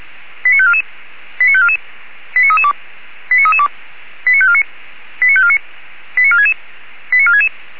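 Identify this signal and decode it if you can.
Five-tone selective call (ZVEI), heard as quick runs of five tones, tones of about 70 ms. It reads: 89544 89544 89332 89332 89549 89549 89466 89466
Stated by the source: ZVEI five-tone call received by SDR, archived by Feuerwehr Orlamünde (fire brigade).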